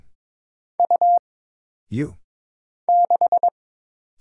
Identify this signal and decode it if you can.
Morse code, heard as a keyed tone, about 23 wpm, 700 Hz. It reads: U 6
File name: Single Character (Letter:Number:Punctuation:Prosign) - Mind-Melt - 22wpm